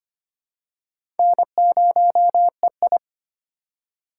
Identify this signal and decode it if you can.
Morse code, heard as a keyed tone, about 25 wpm, 700 Hz. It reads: N0EI
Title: USA Call Signs - 4x Character Spacing - 25wpm